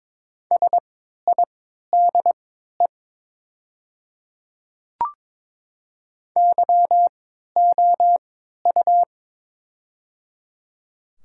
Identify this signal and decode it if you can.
Morse code, heard as a keyed tone, about 22 wpm, spaced as for 12 wpm, 700 Hz. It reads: SIDE YOU
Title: 3x Character Spacing - Top 100 Words - 22wpm